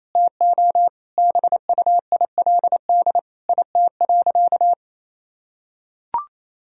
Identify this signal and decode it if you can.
Morse code, heard as a keyed tone, about 28 wpm, 700 Hz. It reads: TO BUILD IT.
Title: Sentences from Top 1000 Words - Review Part 1 - 28wpm